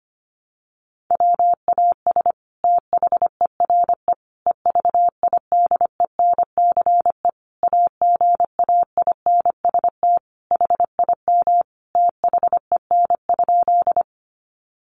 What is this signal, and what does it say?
Morse code, heard as a keyed tone, about 25 wpm, 700 Hz. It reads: WAS THERE EVIDENCE AGAINST HIM THEN?
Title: Sentences from Top 1300 Words 25wpm